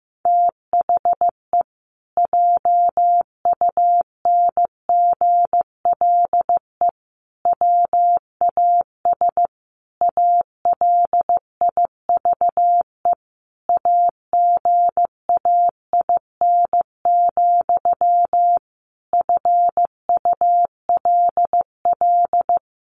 Morse code, heard as a keyed tone, about 15 wpm, 700 Hz. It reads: THE JUNGLE WAS ALIVE AGAIN, FULL
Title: A Sound Of Thunder 15wpm